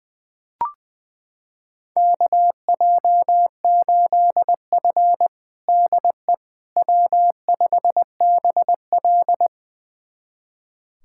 Morse code, heard as a keyed tone, about 20 wpm, 700 Hz. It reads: KJ8F DE W5BL
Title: USA Callsign DE Callsign - 4 Characters - 20wpm